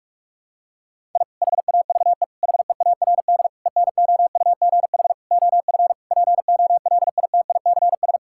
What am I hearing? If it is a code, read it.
Morse code, heard as a keyed tone, about 41 wpm, 700 Hz. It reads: I HAVE HEARD ENOUGH OF POLITICS
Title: Sentences from Top 1000 Words - Review Part 6 - 45wpm